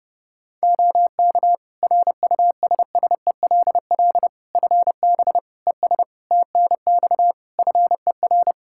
Morse code, heard as a keyed tone, about 30 wpm, 700 Hz. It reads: OK RUSSELL FB ES TNX FER
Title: Practice QSO - Exchange 2 - Info - 30wpm